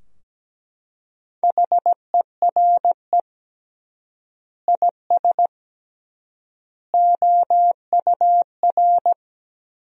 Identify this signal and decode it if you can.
Morse code, heard as a keyed tone, about 17 wpm, 700 Hz. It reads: HERE IS OUR
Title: 3x Word-Spacing - Sentences from Top 500 Words - 17wpm